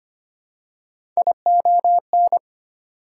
Morse code, heard as a keyed tone, about 25 wpm, 700 Hz. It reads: ION
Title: Prefixes and Suffixes 25wpm